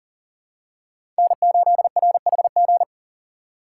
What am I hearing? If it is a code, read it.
Morse code, heard as a keyed tone, about 40 wpm, 700 Hz. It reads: N8RHG
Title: USA Call Signs - 5 Characters - 3x Character Spacing - 40wpm